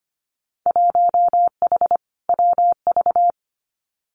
Morse code, heard as a keyed tone, about 25 wpm, 700 Hz. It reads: 1H WV
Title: Winter Field Day Exchanges 25wpm